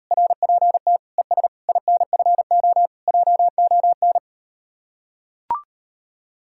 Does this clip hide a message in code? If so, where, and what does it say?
Yes: Morse code from 0.1 to 4.2 s, heard as a keyed tone, about 38 wpm, 700 Hz. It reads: RPT ES INFO JON